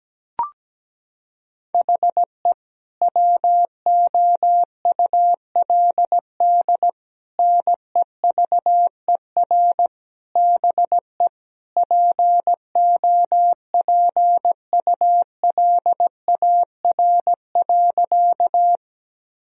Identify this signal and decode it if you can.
Morse code, heard as a keyed tone, about 17 wpm, 700 Hz. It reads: HE WOULD NEVER BE POPULAR.
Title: Sentences from Top 1000 Words - Review Part 3 - 17wpm